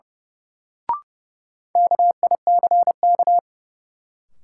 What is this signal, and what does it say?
Morse code, heard as a keyed tone, about 30 wpm, 700 Hz. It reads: KICK
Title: Sentences from Top 1300 Words - New Word Drill - 30wpm